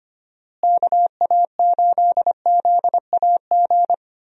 Morse code, heard as a keyed tone, about 25 wpm, 700 Hz. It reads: KA8ZAG
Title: USA Call Signs - 6 Characters - Repeated 2x - 25wpm